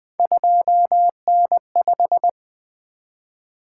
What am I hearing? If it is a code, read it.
Morse code, heard as a keyed tone, about 20 wpm, 700 Hz. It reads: 2N5
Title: Three Letter-Number Combinations - 20wpm